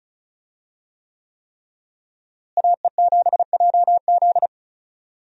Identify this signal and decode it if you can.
Morse code, heard as a keyed tone, about 36 wpm, 700 Hz. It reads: AE7JZ
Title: USA Call Signs - 5 Characters - Repeated 3x - 35wpm